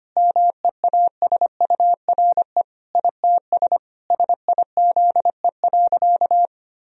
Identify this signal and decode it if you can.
Morse code, heard as a keyed tone, about 25 wpm, 700 Hz. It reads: MEASURE ITS SIZE.
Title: Sentences from Top 1000 Words - Review Part 2 - 25wpm